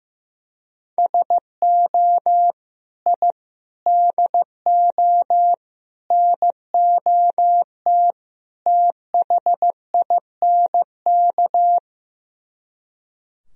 Morse code, heard as a keyed tone, about 15 wpm, 700 Hz. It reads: SO I DO NOT THINK
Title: Speed-Racing - Sets of 5 Words 15wpm